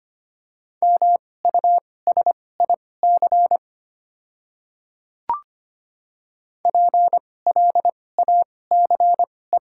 Morse code, heard as a keyed tone, about 25 wpm, spaced as for 18 wpm, 690 Hz. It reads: MUSIC PLACE
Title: 2x Character Spacing - Top 300 Words - 25wpm